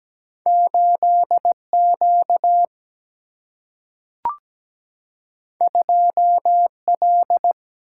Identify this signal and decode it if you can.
Morse code, heard as a keyed tone, about 17 wpm, 700 Hz. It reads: 8 Q 2 L
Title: Single-Letter Single-Number 17wpm